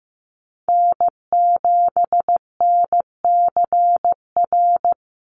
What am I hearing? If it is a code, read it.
Morse code, heard as a keyed tone, about 15 wpm, 700 Hz. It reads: N7NCR
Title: USA Call Signs 15wpm